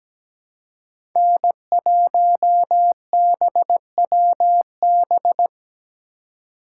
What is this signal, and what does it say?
Morse code, heard as a keyed tone, about 17 wpm, 700 Hz. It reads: N1BWB
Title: USA Call Signs - 4x Character Spacing - 17wpm